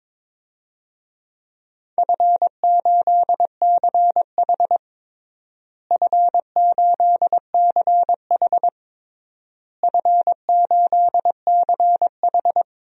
Morse code, heard as a keyed tone, about 22 wpm, 700 Hz. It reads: F8CH F8CH F8CH
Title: DXCC-100 Call Signs - 4 Characters - Repeated 3x - 22wpm